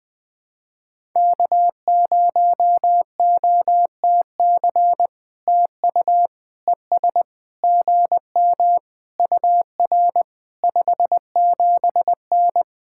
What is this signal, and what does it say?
Morse code, heard as a keyed tone, about 20 wpm, 700 Hz. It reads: K0OTC TU ES GM UR 57N
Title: POTA QSO Practice - 20wpm